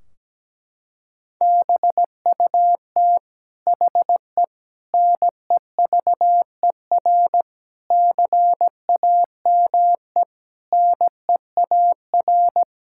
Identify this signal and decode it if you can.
Morse code, heard as a keyed tone, about 17 wpm, 700 Hz. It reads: BUT HE NEVER CAME NEAR
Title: Sentences from Top 1600 Words 17wpm